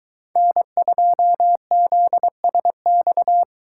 Morse code, heard as a keyed tone, about 21 wpm, 700 Hz. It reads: N2ZSX